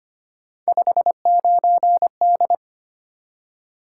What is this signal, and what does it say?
Morse code, heard as a keyed tone, about 25 wpm, 700 Hz. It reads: 59D
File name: Three Letter-Number Combinations - 25wpm